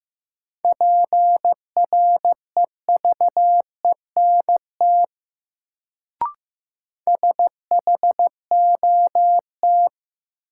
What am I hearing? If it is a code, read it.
Morse code, heard as a keyed tone, about 15 wpm, 700 Hz. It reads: PREVENT SHOT
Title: Sentences from Top 1100 Words - New Word Drill - 15wpm